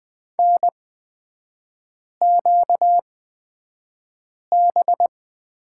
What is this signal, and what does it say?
Morse code, heard as a keyed tone, about 20 wpm, spaced as for 5 wpm, 700 Hz. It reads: NQB
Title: 243 Lesson 39e - Letter z - Three Characters 20wpm